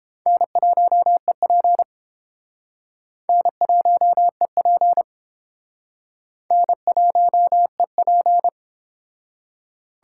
Morse code, heard as a keyed tone, about 30 wpm, 700 Hz. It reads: N1EP N1EP N1EP